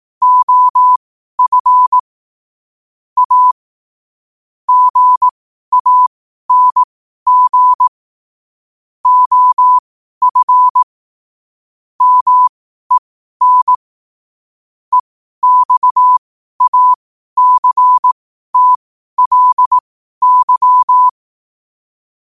Morse code, heard as a keyed tone, about 18 wpm, spaced as for 12 wpm, 1000 Hz. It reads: OF A GANG OF MEN EXACTLY
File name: qotdfiles / 12wpm